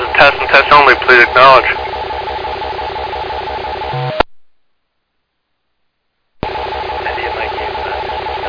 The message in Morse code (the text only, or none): none